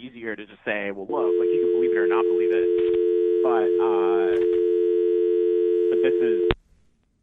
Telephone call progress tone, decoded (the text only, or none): dial tone